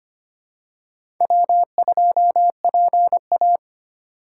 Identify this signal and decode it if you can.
Morse code, heard as a keyed tone, about 25 wpm, 700 Hz. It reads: W2PA
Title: USA Call Signs - 4 Characters - 2x Character Spacing - 25wpm